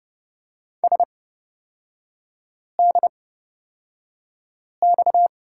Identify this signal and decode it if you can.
Morse code, heard as a keyed tone, about 30 wpm, spaced as for 5 wpm, 700 Hz. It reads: SDX
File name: Lesson 38e - Letter q - Three Characters 30wpm